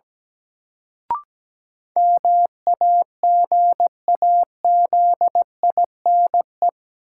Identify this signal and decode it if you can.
Morse code, heard as a keyed tone, about 17 wpm, 700 Hz. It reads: MAGAZINE
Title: Sentences from Top 1000 Words - Review Part 7 - New Word Drill - 17wpm